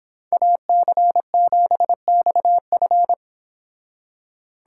Morse code, heard as a keyed tone, about 26 wpm, 700 Hz. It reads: AC7XF